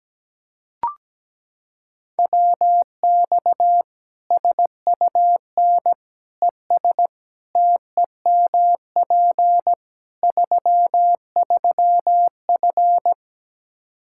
Morse code, heard as a keyed tone, about 17 wpm, 700 Hz. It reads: WX SUN ES TEMP 33F
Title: K1USN SST QSO Practice 17wpm